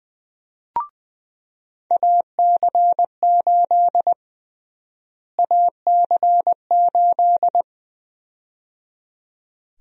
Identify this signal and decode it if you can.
Morse code, heard as a keyed tone, about 21 wpm, 700 Hz. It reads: AC8 AC8
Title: USA Call Sign Prefixes - Repeated 2x - 20wpm